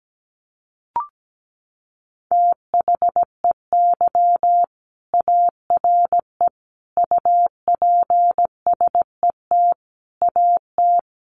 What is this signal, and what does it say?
Morse code, heard as a keyed tone, about 17 wpm, 700 Hz. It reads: THEY ARE UPSET AT
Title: Sentences from Top 800 Words 17wpm